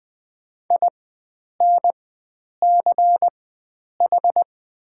Morse code, heard as a keyed tone, about 20 wpm, spaced as for 9 wpm, 700 Hz. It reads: INCH
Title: 4x Character Spacing - Top 500 Words - 20wpm